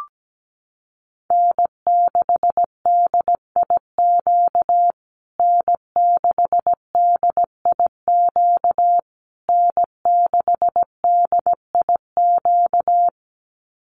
Morse code, heard as a keyed tone, about 17 wpm, 700 Hz. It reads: N6DIQ N6DIQ N6DIQ
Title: USA Call Signs Repeated 3x 17wpm